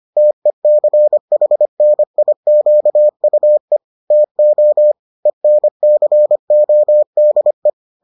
Morse code, heard as a keyed tone, about 25 wpm, 600 Hz. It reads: TECHNIQUE TO ENCODE